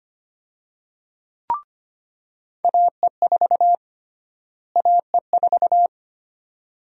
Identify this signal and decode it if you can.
Morse code, heard as a keyed tone, about 25 wpm, 700 Hz. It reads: AE4 AE4
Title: USA Call Sign Prefixes - Repeated 2x - 25wpm